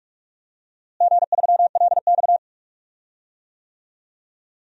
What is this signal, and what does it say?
Morse code, heard as a keyed tone, about 41 wpm, 700 Hz. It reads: G3LX
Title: DXCC-100 Call Signs - 4 Characters - Repeated 3x - 45wpm